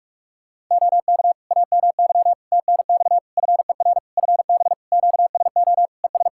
Morse code, heard as a keyed tone, about 41 wpm, 700 Hz. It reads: OK AMY TNX FER FB QSO ES
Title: Practice QSO - Exchange 4 - Ending - 45wpm